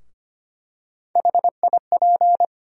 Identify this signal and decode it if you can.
Morse code, heard as a keyed tone, about 25 wpm, 700 Hz. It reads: HIP
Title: USA Call Sign Postfixes - 3 Letters 25wpm